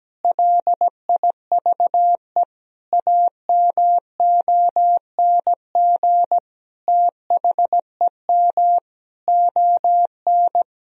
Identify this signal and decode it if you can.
Morse code, heard as a keyed tone, about 17 wpm, 700 Hz. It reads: LIVE AMONG THEM ON
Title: Sentences from Top 1000 Words - Review Part 3 - 17wpm